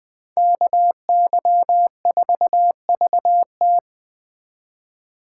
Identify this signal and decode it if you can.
Morse code, heard as a keyed tone, about 20 wpm, 700 Hz. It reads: KY4VT